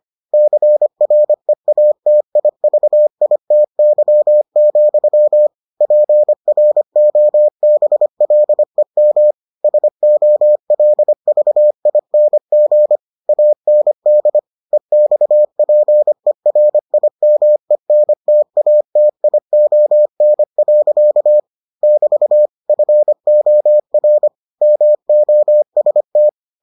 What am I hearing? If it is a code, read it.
Morse code, heard as a keyed tone, about 25 wpm, 600 Hz. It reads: CREATIVITY, PROBLEM SOLVING AND EXPERIMENTATION. <BT> FOR MOST